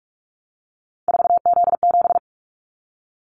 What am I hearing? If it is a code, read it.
Morse code, heard as a keyed tone, about 46 wpm, 700 Hz. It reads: 4Z7
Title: Callsign Prefixes for Top 50 DXCC - 45wpm